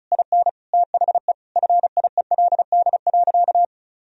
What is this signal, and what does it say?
Morse code, heard as a keyed tone, about 36 wpm, 700 Hz. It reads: IN THE FIELD.
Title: Sentences from Top 1000 Words - Review Part 4 - 35wpm